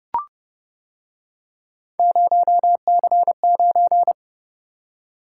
Morse code, heard as a keyed tone, about 29 wpm, 700 Hz. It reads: 0C9